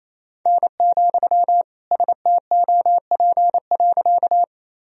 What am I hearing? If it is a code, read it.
Morse code, heard as a keyed tone, about 28 wpm, 700 Hz. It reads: N, STOP.
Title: Sentences from Top 300 Words and Abbreviations 28wpm